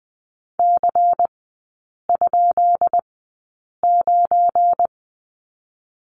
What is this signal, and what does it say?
Morse code, heard as a keyed tone, about 20 wpm, spaced as for 8 wpm, 700 Hz. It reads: C?9
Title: 205 Lesson 33e - Letter j - Three Characters 20wpm